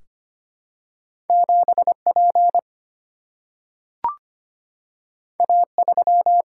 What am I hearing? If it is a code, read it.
Morse code, heard as a keyed tone, about 25 wpm, 700 Hz. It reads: 7P A3
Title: Single-Letter Single-Number 25wpm